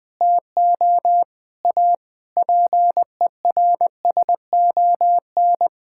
Morse code, heard as a keyed tone, about 20 wpm, 700 Hz. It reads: TO A PERSON
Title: Sentences with 7-Word Phrases - 20wpm